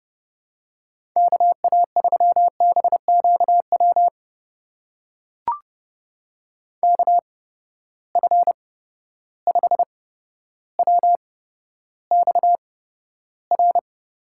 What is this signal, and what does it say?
Morse code, heard as a keyed tone, about 30 wpm, spaced as for 13 wpm, 700 Hz. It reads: KA3BQW KF5WXR
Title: USA Call Signs - 6 Characters - 8x Character Spacing - 30wpm